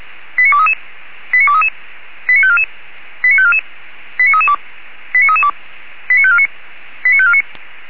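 Five-tone selective call (ZVEI), heard as quick runs of five tones, tones of about 70 ms. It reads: 89233 89233 89544 89544 89332 89332 89549 89549
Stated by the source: ZVEI five-tone selective call 89386 received with an SDR.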